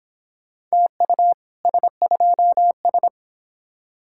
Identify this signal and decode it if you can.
Morse code, heard as a keyed tone, about 26 wpm, 700 Hz. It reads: TU S2S